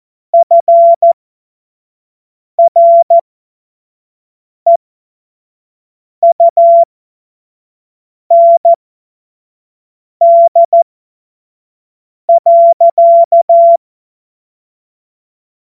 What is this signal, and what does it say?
Morse code, heard as a keyed tone, about 14 wpm, spaced as for 5 wpm, 680 Hz. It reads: FREUND.
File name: Morsekurs / Witze / Witz 15. November 2022 25 F70